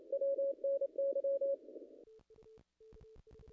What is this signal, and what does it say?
Morse code, heard as a keyed tone, about 28 wpm, 550 Hz. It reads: WNY
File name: contests / ss-cw-k5tr-op-n5rz -0100